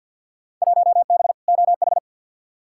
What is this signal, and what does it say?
Morse code, heard as a keyed tone, about 51 wpm, 700 Hz. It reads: 1B OH
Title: Field Day Exchanges - 50wpm